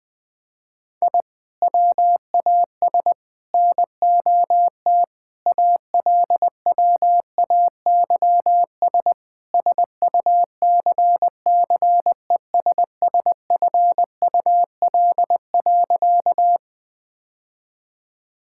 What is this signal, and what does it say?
Morse code, heard as a keyed tone, about 21 wpm, 700 Hz. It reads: I WAS NOT ALWAYS SUCCESSFUL.
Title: Sentences from Top 1000 Words - Review Part 6 - 20wpm